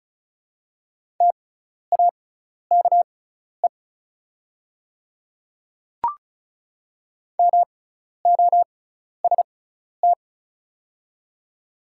Morse code, heard as a keyed tone, about 34 wpm, spaced as for 12 wpm, 700 Hz. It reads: TAKE MOST
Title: 6x Character Spacing - Top 300 Words - 35wpm